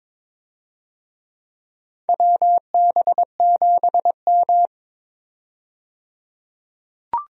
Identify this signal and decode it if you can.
Morse code, heard as a keyed tone, about 22 wpm, 700 Hz. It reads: WB7M
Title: USA Call Signs - 4 Characters - 22wpm